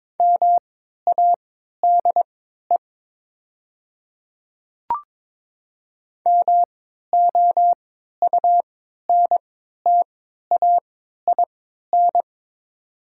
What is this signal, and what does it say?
Morse code, heard as a keyed tone, about 22 wpm, spaced as for 12 wpm, 700 Hz. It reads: MADE MOUNTAIN
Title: 3x Character Spacing - Top 300 Words - 22wpm